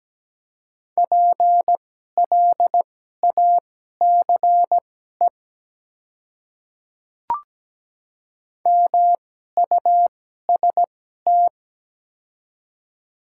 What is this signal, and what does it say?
Morse code, heard as a keyed tone, about 17 wpm, spaced as for 12 wpm, 700 Hz. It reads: PLACE MUST
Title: 2x Character Spacing - Top 200 Words - 17wpm